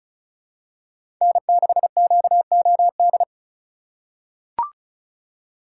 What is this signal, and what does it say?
Morse code, heard as a keyed tone, about 36 wpm, 700 Hz. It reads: N6QOD
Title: USA Call Signs - 5 Characters - 4x Character Spacing - 35wpm